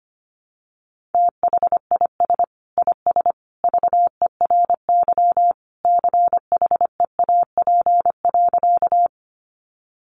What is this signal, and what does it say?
Morse code, heard as a keyed tone, about 25 wpm, 700 Hz. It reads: THIS IS VERY CHEAP.
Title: Sentences from Top 1200 Words 25wpm